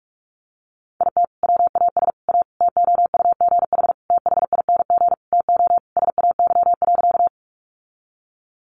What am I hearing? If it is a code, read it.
Morse code, heard as a keyed tone, about 44 wpm, 700 Hz. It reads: IT WAS A TOUGH THING TO SAY.
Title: Sentences from Top 1200 Words 45wpm